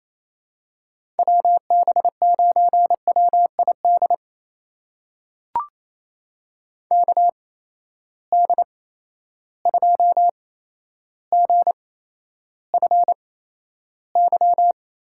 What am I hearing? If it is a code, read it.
Morse code, heard as a keyed tone, about 28 wpm, spaced as for 12 wpm, 700 Hz. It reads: WB9WID KD2GFY
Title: USA Call Signs - 6 Characters - 8x Character Spacing - 28wpm